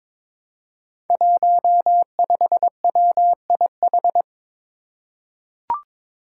Morse code, heard as a keyed tone, about 22 wpm, 700 Hz. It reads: 15WIH
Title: Five Challenging Letter-Number Combinations 22wpm